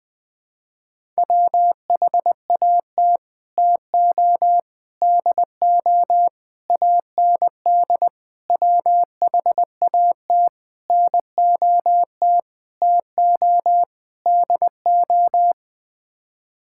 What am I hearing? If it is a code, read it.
Morse code, heard as a keyed tone, about 20 wpm, 700 Hz. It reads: WHAT TO DO AND WHAT NOT TO DO
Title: Sets of 8 Words - 20wpm